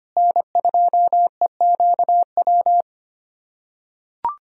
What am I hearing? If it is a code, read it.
Morse code, heard as a keyed tone, about 25 wpm, 700 Hz. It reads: N2EQW
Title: USA Callsign DE Callsign - 5 Characters - 25wpm